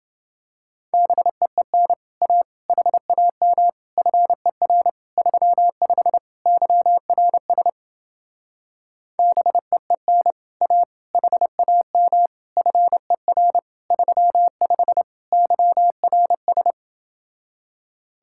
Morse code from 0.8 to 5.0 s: BEEN A HAM FER

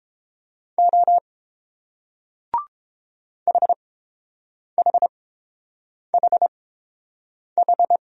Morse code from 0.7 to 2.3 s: O

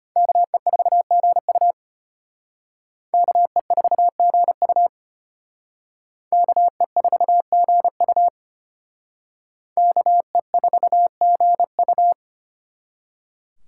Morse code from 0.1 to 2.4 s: KE4GU

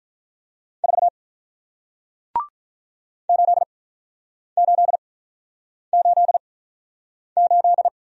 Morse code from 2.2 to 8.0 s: 8888

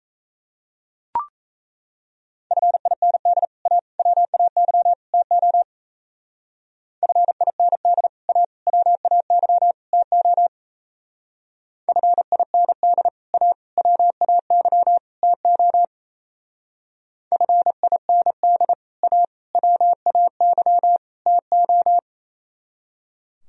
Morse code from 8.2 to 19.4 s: A WAY TO FIND A WAY TO FIND A

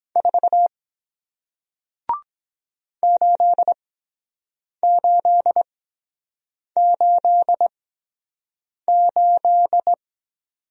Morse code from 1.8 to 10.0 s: 8888

